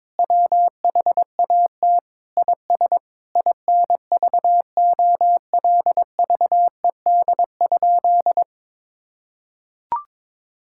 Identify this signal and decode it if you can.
Morse code, heard as a keyed tone, about 22 wpm, 700 Hz. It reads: WHAT IS INVOLVED?
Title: Sentences from Top 1000 Words - Review Part 5 - 22wpm